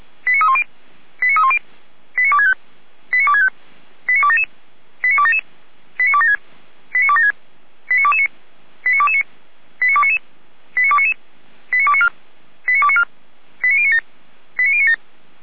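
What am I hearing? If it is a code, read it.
Five-tone selective call (ZVEI), heard as quick runs of five tones, tones of about 70 ms. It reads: 89310 89310 89265 89265 89277 89277 89276 89276 89229 89229 89299 89299 89294 89294 89097 89097
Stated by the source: ZVEI five-tone call received with an SDR, from the Feuerwehr Orlamünde audio server.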